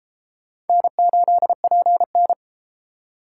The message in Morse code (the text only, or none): N8PN